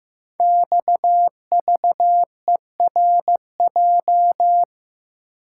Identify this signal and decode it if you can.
Morse code, heard as a keyed tone, about 15 wpm, 700 Hz. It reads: XVERJ
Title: Code Groups - 5 Letters - 15wpm